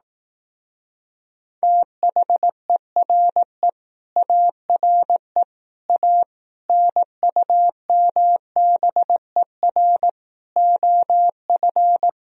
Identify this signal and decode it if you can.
Morse code, heard as a keyed tone, about 18 wpm, 700 Hz. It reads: THERE ARE A NUMBER OF
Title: Speed-Racing - Sets of 5 Words 15wpm